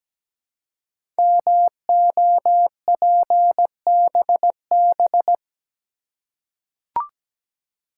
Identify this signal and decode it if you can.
Morse code, heard as a keyed tone, about 17 wpm, 700 Hz. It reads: MOPBB
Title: Code Groups - 5 Letters - 17wpm